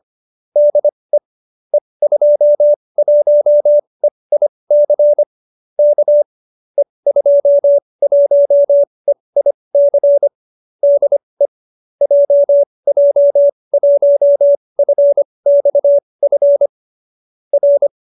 Morse code, heard as a keyed tone, about 25 wpm, spaced as for 20 wpm, 570 Hz. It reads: DE E21EIC K E21EIC DE JJ1FXF R
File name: qso208b 20